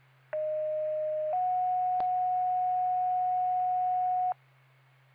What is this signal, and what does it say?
Two-tone page, heard as a one-second tone, then a three-second tone, about 620 then 730 Hz. It is heard from 0.3 to 4.3 s.
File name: Stanhope FD